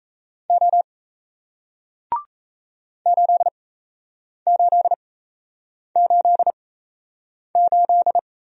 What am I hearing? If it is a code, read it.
Morse code, heard as a keyed tone, about 38 wpm, spaced as for 8 wpm, 700 Hz. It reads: O 8888